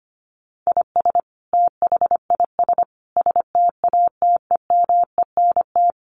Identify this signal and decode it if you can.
Morse code, heard as a keyed tone, about 25 wpm, 700 Hz. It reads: IS THIS STATEMENT
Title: Sentences from Top 1200 Words 25wpm